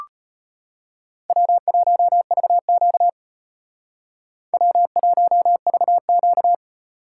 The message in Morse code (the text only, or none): W1VQ W1VQ